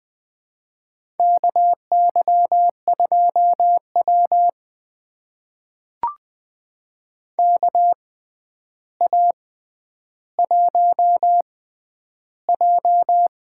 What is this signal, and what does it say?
Morse code, heard as a keyed tone, about 20 wpm, spaced as for 10 wpm, 700 Hz. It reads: KY2W KA1J
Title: USA Call Signs - 6x Character Spacing - 20wpm